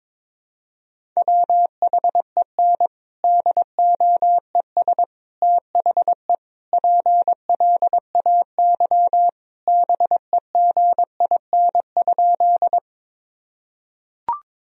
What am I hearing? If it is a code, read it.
Morse code, heard as a keyed tone, about 22 wpm, 700 Hz. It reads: WHEN DOES THE PLAY BEGIN?
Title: Sentences from Top 1000 Words - Review Part 1 - 22wpm